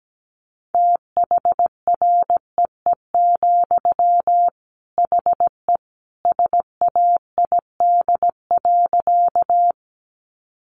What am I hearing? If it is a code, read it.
Morse code, heard as a keyed tone, about 17 wpm, 700 Hz. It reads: THREE, HE SAID.
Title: Sentences from Top 200 Words 17wpm